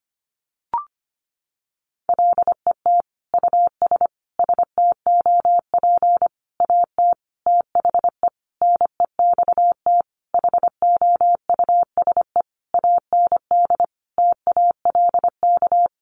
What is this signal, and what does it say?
Morse code, heard as a keyed tone, about 25 wpm, 700 Hz. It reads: LET US STOP AT THE NEXT HOUSE AND TALK
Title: Sentences from Top 400 Words 25wpm